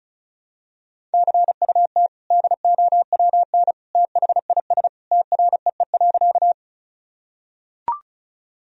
Morse code, heard as a keyed tone, about 34 wpm, 690 Hz. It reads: CUT DOWN THIS TREE.
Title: Sentences from Top 1000 Words - Review Part 1 - 35wpm